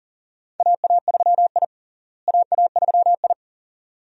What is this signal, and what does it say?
Morse code, heard as a keyed tone, about 40 wpm, 700 Hz. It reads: AA3I AA3I
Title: USA Call Signs - 4 Characters - Repeated 2x - 40wpm